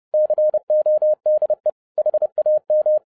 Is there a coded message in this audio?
Yes: Morse code from 0.1 to 3.0 s, heard as a keyed tone, about 30 wpm, 610 Hz. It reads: CODE HAM